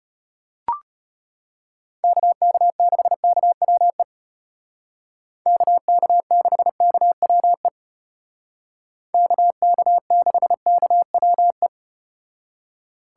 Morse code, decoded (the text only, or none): KK6KWE KK6KWE KK6KWE